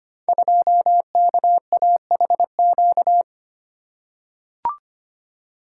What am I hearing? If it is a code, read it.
Morse code, heard as a keyed tone, about 25 wpm, 700 Hz. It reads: 2KAHQ
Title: Code Groups - 5 Letter:Numbers - 25wpm